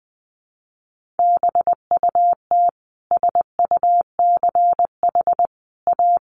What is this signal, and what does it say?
Morse code, heard as a keyed tone, about 20 wpm, 700 Hz. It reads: BUT SUCH A
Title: Sentences from Top 1500 Words 20wpm